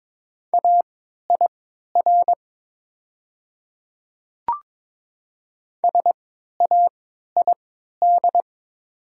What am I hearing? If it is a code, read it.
Morse code, heard as a keyed tone, about 22 wpm, spaced as for 13 wpm, 700 Hz. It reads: AIR SAID
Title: 3x Character Spacing - Top 200 Words - 22wpm